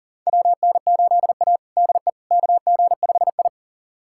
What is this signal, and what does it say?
Morse code, heard as a keyed tone, about 40 wpm, 700 Hz. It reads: WN8A DE KG5I
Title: USA Callsign DE Callsign - 4 Characters - 40wpm